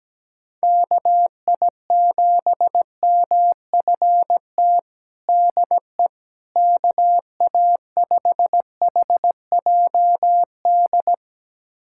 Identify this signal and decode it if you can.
Morse code, heard as a keyed tone, about 17 wpm, 700 Hz. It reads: KI7MFT DE KA5HJD
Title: USA Callsign DE Callsign - 6 Characters - 17wpm